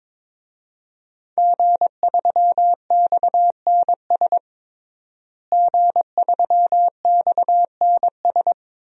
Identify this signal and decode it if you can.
Morse code, heard as a keyed tone, about 22 wpm, 700 Hz. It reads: G3XNS G3XNS
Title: DXCC-100 Call Signs - 5 Characters - Repeated 3x - 22wpm